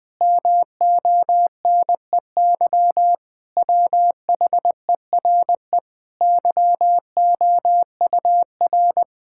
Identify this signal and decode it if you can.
Morse code, heard as a keyed tone, about 20 wpm, 700 Hz. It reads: MONEY WHERE YOUR
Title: Sets of 8 Words - 20wpm